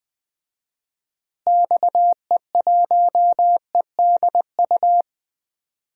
Morse code, heard as a keyed tone, about 20 wpm, 700 Hz. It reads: XE1EDU
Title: DXCC-100 Call Signs - 6 Characters - Repeated 3x - 20wpm